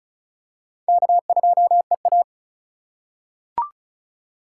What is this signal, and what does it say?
Morse code, heard as a keyed tone, about 34 wpm, 700 Hz. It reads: K2EA
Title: USA Call Signs - 4x Character Spacing - 35wpm